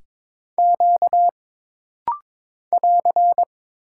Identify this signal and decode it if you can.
Morse code, heard as a keyed tone, about 22 wpm, 700 Hz. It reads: Q <AR>
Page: This rapid-fire timing is geared to push your ICR (Instant Character Recognition) skills.